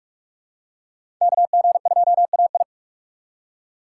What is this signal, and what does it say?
Morse code, heard as a keyed tone, about 46 wpm, 700 Hz. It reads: KG2AI
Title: USA Call Signs - 5 Characters - 6x Character Spacing - 45wpm